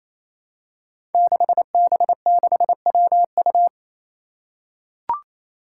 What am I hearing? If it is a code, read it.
Morse code, heard as a keyed tone, about 28 wpm, 700 Hz. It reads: 6B6WU
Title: Five Challenging Letter-Number Combinations 28wpm